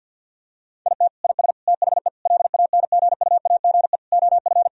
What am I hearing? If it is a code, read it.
Morse code, heard as a keyed tone, about 50 wpm, 700 Hz. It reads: IT IS THE LANGUAGE OF
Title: Sentences from Top 1000 Words - Review Part 3 - 50wpm